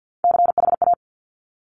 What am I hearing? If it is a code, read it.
Morse code, heard as a keyed tone, about 50 wpm, 700 Hz. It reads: CHA
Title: Top Three Letter Combinations in Words 50wpm